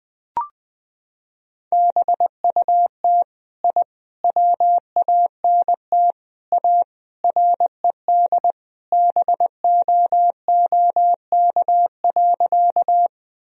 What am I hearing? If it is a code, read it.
Morse code, heard as a keyed tone, about 21 wpm, 700 Hz. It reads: BUT I WANT A RED BOOK.
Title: Sentences from Top 1000 Words - Review Part 2 - 20wpm